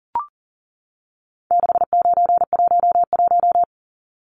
Morse code, heard as a keyed tone, about 40 wpm, 700 Hz. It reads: 6911
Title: Numbers - 4-Digit 40wpm